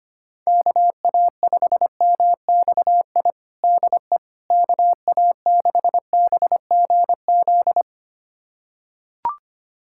Morse code, heard as a keyed tone, about 25 wpm, 700 Hz. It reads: KA5MXI DE KA6BGZ
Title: USA Callsign DE Callsign - 6 Characters - 25wpm